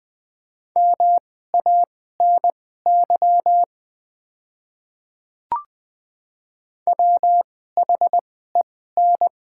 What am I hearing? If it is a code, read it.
Morse code, heard as a keyed tone, about 20 wpm, spaced as for 14 wpm, 700 Hz. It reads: MANY WHEN
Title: 2x Character Spacing - Top 200 Words - 20wpm